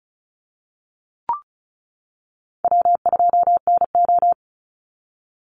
Morse code, heard as a keyed tone, about 36 wpm, 700 Hz. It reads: W2NO
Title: Call Signs - 35wpm